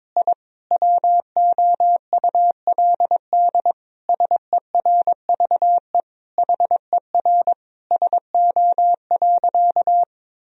Morse code, heard as a keyed tone, about 22 wpm, 700 Hz. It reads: I WOULD SERVE HER SO.
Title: Sentences from Top 1000 Words - Review Part 2 - 22wpm